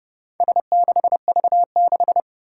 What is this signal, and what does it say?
Morse code, heard as a keyed tone, about 31 wpm, 700 Hz. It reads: S6V6
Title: Four Challenging Letter-Number Combinations 30wpm